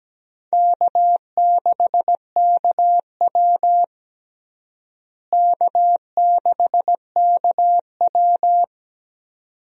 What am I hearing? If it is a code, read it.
Morse code, heard as a keyed tone, about 17 wpm, 700 Hz. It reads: K6KW K6KW
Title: USA Call Signs - 4 Characters - Repeated 2x - 17wpm